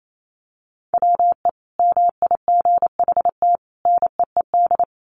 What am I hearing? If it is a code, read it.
Morse code, heard as a keyed tone, about 28 wpm, 700 Hz. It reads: WE MIGHT NEED
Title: A Sound Of Thunder 28wpm